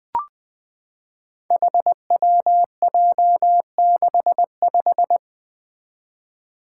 Morse code, heard as a keyed tone, about 20 wpm, 700 Hz. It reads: HWJ65
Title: Five Challenging Letter-Number Combinations 20wpm